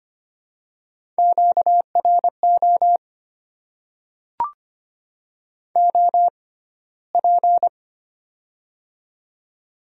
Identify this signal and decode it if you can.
Morse code, heard as a keyed tone, about 25 wpm, spaced as for 15 wpm, 700 Hz. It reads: QRO OP
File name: Top 90 QSO Elements - 6x Character Spacing - 25wpm